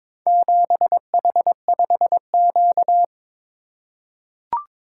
Morse code, heard as a keyed tone, about 22 wpm, 700 Hz. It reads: 7H5Q
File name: practice / 22wpm / Four Letter-Number Combinations - 22wpm